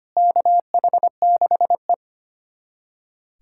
Morse code, heard as a keyed tone, about 25 wpm, 700 Hz. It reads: KH6E
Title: USA Callsign DE Callsign - 4 Characters - 25wpm